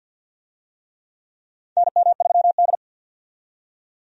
Morse code, heard as a keyed tone, about 48 wpm, 700 Hz. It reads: NM3D
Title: USA Call Signs - 4 Characters - 50wpm